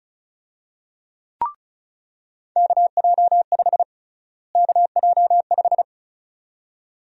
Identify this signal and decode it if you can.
Morse code, heard as a keyed tone, about 34 wpm, 700 Hz. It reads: KJ5 KJ5
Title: USA Call Sign Prefixes - Repeated 2x - 35wpm